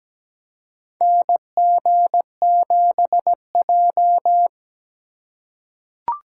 Morse code, heard as a keyed tone, about 17 wpm, 700 Hz. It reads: NG7J
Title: USA Call Signs - 2x Character Spacing - 17wpm